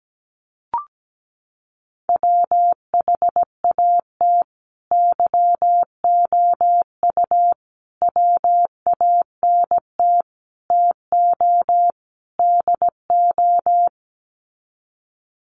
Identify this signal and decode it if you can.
Morse code, heard as a keyed tone, about 17 wpm, 700 Hz. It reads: WHAT YOU WANT TO DO
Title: Sets of 5 Words Encore 17wpm